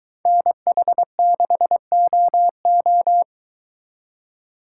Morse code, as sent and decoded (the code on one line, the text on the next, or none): -. .... -.... --- ---
NH6OO